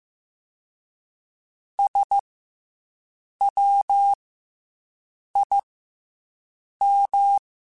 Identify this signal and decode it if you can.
Morse code, heard as a keyed tone, about 15 wpm, 800 Hz. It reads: S W I M
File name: Special INGs-mp3s / Special INGs-mp3s / Fast ing @13